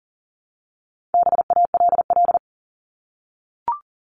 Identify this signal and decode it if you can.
Morse code, heard as a keyed tone, about 40 wpm, 700 Hz. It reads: BALL
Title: Top 751-1000 Words 40wpm